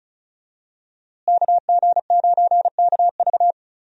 Morse code, heard as a keyed tone, about 36 wpm, 700 Hz. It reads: KG9KV
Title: USA Call Signs - 5 Characters - Repeated 2x - 35wpm